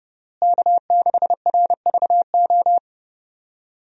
Morse code, heard as a keyed tone, about 32 wpm, 700 Hz. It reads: K6RVO
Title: USA Call Signs - 5 Characters - Repeated 3x - 30wpm